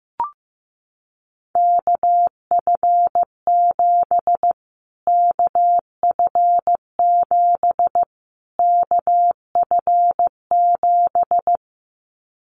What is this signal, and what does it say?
Morse code, heard as a keyed tone, about 15 wpm, 700 Hz. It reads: KF7 KF7 KF7
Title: Callsign Prefixes for Top 25 DXCC - Repeated 3x - 15wpm